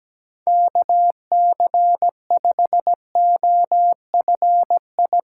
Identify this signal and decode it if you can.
Morse code, heard as a keyed tone, about 17 wpm, 700 Hz. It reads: KC5OFI
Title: USA Call Signs - 6 Characters - 3x Character Spacing - 17wpm